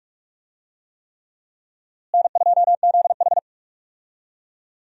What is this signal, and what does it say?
Morse code, heard as a keyed tone, about 42 wpm, 700 Hz. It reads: N2ZH
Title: USA Call Signs - 4 Characters - Repeated 3x - 45wpm